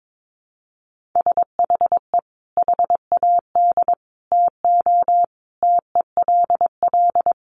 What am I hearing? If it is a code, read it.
Morse code, heard as a keyed tone, about 22 wpm, 700 Hz. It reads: SHE HAD TO TELL
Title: Sentences from Top 200 Words 22wpm